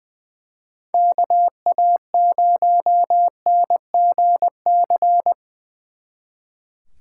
Morse code, heard as a keyed tone, about 23 wpm, 700 Hz. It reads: KA0NGC
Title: Speed Racing - USA Call Signs 20wpm